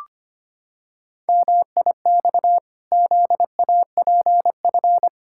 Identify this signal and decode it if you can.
Morse code, heard as a keyed tone, about 25 wpm, 700 Hz. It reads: MIX ZAPF
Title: Pangram Challenge v2 - 25wpm